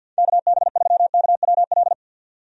Morse code, heard as a keyed tone, about 50 wpm, 700 Hz. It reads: KB3KWL
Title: USA Call Signs - 6 Characters - 50wpm